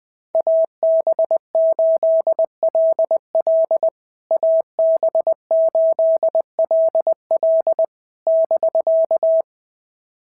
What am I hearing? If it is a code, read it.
Morse code, heard as a keyed tone, about 20 wpm, 650 Hz. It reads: AB8LL AB8LL <BK>
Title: POTA QSO Practice - 20wpm